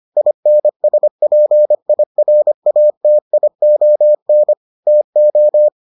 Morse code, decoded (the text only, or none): INSPIRATION TO